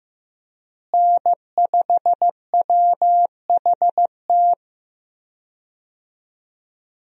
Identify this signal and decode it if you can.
Morse code, heard as a keyed tone, about 16 wpm, 700 Hz. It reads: N5WHT
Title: USA Call Signs - 5 Characters - Repeated 2x - 15wpm